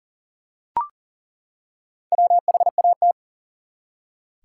Morse code, heard as a keyed tone, about 38 wpm, 700 Hz. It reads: WHAT
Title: Top 180 QSO Elements 40wpm